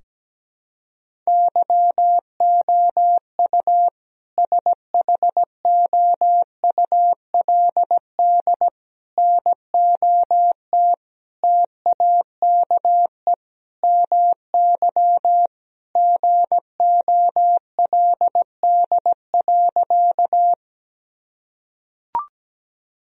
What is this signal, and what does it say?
Morse code, heard as a keyed tone, about 17 wpm, 700 Hz. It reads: YOU SHOULD NOT TAKE MY GOLD.
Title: Sentences from Top 1000 Words - Review Part 3 - 17wpm